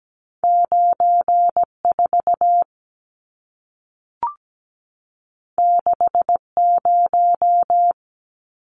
Morse code, heard as a keyed tone, about 17 wpm, 700 Hz. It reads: 94 60
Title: Numbers - 2-Digit 17wpm